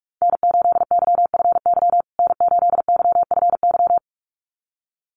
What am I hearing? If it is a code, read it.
Morse code, heard as a keyed tone, about 44 wpm, 700 Hz. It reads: N8YFY N8YFY
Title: USA Call Signs Repeated 2x 45wpm